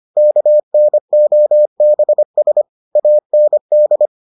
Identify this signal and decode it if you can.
Morse code, heard as a keyed tone, about 25 wpm, 600 Hz. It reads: KNOBS AND